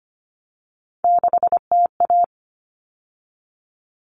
Morse code, heard as a keyed tone, about 25 wpm, 700 Hz. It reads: 6TA
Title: All Cut Numbers - 3-Digit 25wpm